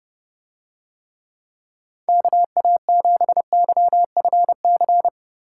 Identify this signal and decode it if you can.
Morse code, heard as a keyed tone, about 30 wpm, 700 Hz. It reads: KA7YFC